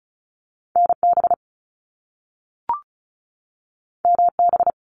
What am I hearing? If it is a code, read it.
Morse code, heard as a keyed tone, about 36 wpm, 700 Hz. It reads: NB MB
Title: Canadian Provinces and Territories 35wpm